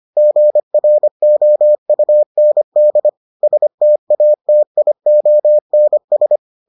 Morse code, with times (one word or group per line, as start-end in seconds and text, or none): 0.2-3.1 GROUND
3.4-6.4 STATIONS